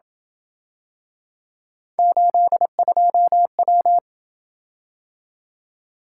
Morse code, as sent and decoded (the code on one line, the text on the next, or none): ---.. ..--- .--
82W